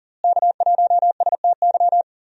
Morse code, heard as a keyed tone, about 40 wpm, 700 Hz. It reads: K1STY
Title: USA Call Signs - 5 Characters - Repeated 3x - 40wpm